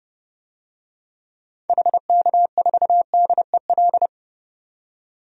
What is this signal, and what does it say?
Morse code, heard as a keyed tone, about 30 wpm, 700 Hz. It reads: HK4DEL
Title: DXCC-100 Call Signs - 6 Characters - 30wpm